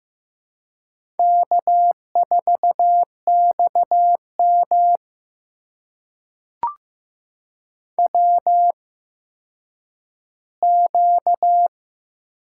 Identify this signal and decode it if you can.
Morse code, heard as a keyed tone, about 15 wpm, spaced as for 9 wpm, 700 Hz. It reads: K4XM WQ
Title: USA Call Signs - 4 Characters - 8x Character Spacing - 15wpm